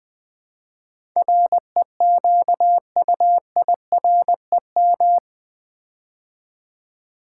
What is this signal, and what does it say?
Morse code, heard as a keyed tone, about 20 wpm, 700 Hz. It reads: REQUIREM
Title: Build A Word - 4 Letters At A Time - 11-Letter Words - 20wpm